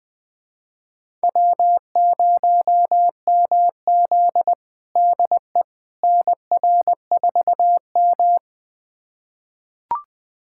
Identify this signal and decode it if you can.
Morse code, heard as a keyed tone, about 20 wpm, 700 Hz. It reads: W0MZ DE NR4M
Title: USA Callsign DE Callsign - 4 Characters - 20wpm